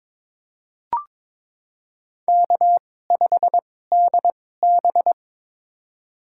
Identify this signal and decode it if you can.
Morse code, heard as a keyed tone, about 22 wpm, spaced as for 16 wpm, 700 Hz. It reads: K5DB
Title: USA Call Signs - 4 Characters - 2x Character Spacing - 22wpm